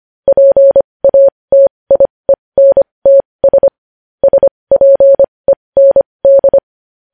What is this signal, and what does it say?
Morse code, heard as a keyed tone, about 25 wpm, spaced as for 20 wpm, 550 Hz. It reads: PATIENTS SPEND
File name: podcast / News headlines-1x-25x20